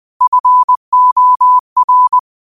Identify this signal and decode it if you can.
Morse code, heard as a keyed tone, about 20 wpm, 1000 Hz. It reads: FOR